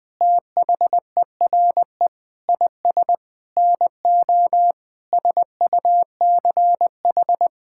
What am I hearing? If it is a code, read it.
Morse code, heard as a keyed tone, about 20 wpm, 700 Hz. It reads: THERE IS NO SUCH
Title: Sentences with 8-Word Phrases - 20wpm